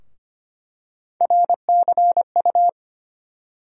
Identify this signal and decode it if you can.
Morse code, heard as a keyed tone, about 25 wpm, 700 Hz. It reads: RCU
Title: USA Call Sign Postfixes - 3 Letters 25wpm